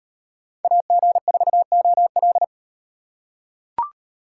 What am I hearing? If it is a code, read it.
Morse code, heard as a keyed tone, about 35 wpm, 700 Hz. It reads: AG4OL